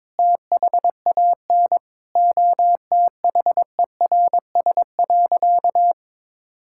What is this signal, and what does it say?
Morse code, heard as a keyed tone, about 22 wpm, 700 Hz. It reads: THAN OTHERS.
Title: Sentences from Top 1000 Words - Review Part 6 - 22wpm